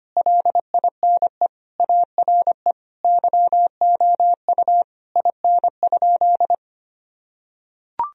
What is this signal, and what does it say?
Morse code, heard as a keyed tone, about 25 wpm, 700 Hz. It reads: LINE ARE YOU IN?
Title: Sentences from Top 1000 Words - Review Part 1 - 25wpm